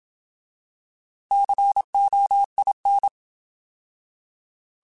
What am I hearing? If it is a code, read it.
Morse code, heard as a keyed tone, about 27 wpm, 800 Hz. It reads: COIN
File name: Alphabetical / Alphabetical mp3s / Cs @25